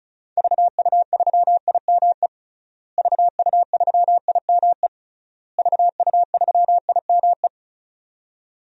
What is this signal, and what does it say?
Morse code, heard as a keyed tone, about 36 wpm, 700 Hz. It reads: VU3IME VU3IME VU3IME
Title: DXCC-100 Call Signs - 6 Characters - Repeated 3x - 35wpm